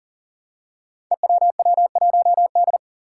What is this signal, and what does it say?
Morse code, heard as a keyed tone, about 41 wpm, 700 Hz. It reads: EWW1D
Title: Five Challenging Letter-Number Combinations 40wpm